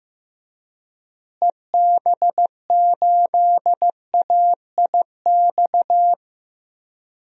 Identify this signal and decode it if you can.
Morse code, heard as a keyed tone, about 15 wpm, 700 Hz. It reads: EB8AIX
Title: DXCC-100 Call Signs - 6 Characters - Repeated 3x - 15wpm